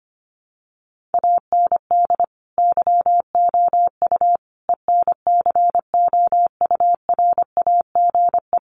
Morse code, heard as a keyed tone, about 25 wpm, 700 Hz. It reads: AND YOU ENCOURAGE
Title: Sentences from Top 1400 Words 25wpm